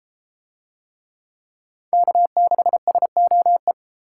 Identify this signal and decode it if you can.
Morse code, heard as a keyed tone, about 29 wpm, 700 Hz. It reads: K6SOE